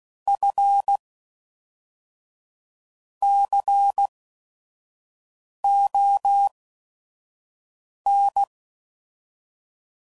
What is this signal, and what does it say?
Morse code, heard as a keyed tone, about 16 wpm, spaced as for 5 wpm, 800 Hz. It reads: F CON
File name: CW Code Groups @5WPM